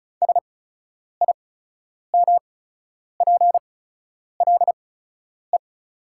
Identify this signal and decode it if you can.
Morse code, heard as a keyed tone, about 36 wpm, spaced as for 10 wpm, 700 Hz. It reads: SIMPLE
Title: 8x Character Spacing - Top 500 Words - 35wpm